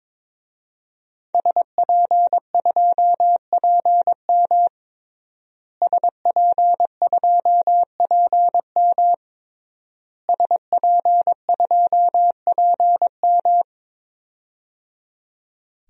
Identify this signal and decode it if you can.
Morse code, heard as a keyed tone, about 22 wpm, 700 Hz. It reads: SP2PM SP2PM SP2PM
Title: DXCC-100 Call Signs - 5 Characters - Repeated 3x - 22wpm